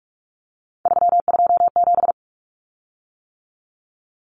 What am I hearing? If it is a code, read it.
Morse code, heard as a keyed tone, about 44 wpm, 700 Hz. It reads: 327
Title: Common Cut Numbers - 3-Digit 45wpm